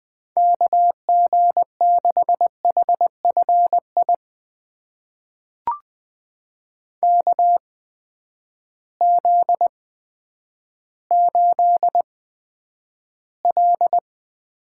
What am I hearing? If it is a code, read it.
Morse code, heard as a keyed tone, about 20 wpm, spaced as for 10 wpm, 700 Hz. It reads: KG6HFI KZ8L
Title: USA Call Signs - 8x Character Spacing - 20wpm